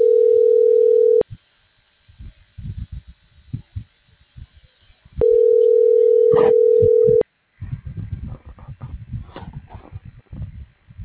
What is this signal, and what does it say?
Ringback tone (North America), 440 + 480 Hz.